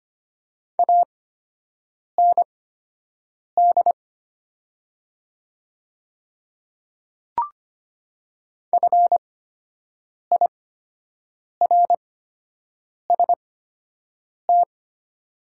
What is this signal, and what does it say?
Morse code, heard as a keyed tone, about 25 wpm, spaced as for 7 wpm, 700 Hz. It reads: AND FIRST
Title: 8x Character Spacing - Top 100 Words - 25wpm